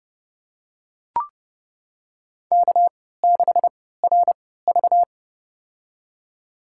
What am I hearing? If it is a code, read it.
Morse code, heard as a keyed tone, about 30 wpm, spaced as for 17 wpm, 700 Hz. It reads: K6RV
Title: USA Call Signs - 4 Characters - 3x Character Spacing - 30wpm